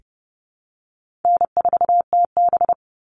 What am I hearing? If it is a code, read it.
Morse code, heard as a keyed tone, about 30 wpm, 700 Hz. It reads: N4TB
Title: Call Signs - 30wpm